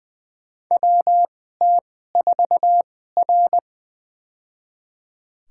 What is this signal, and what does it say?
Morse code, heard as a keyed tone, about 20 wpm, spaced as for 14 wpm, 700 Hz. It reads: WT4R
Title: USA Call Signs - 4 Characters - 2x Character Spacing - 20wpm